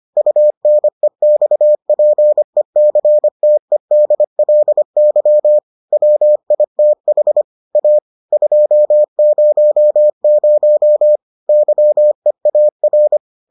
Morse code, heard as a keyed tone, about 25 wpm, 600 Hz. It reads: UNEXPECTEDLY WITH A 200 YEAR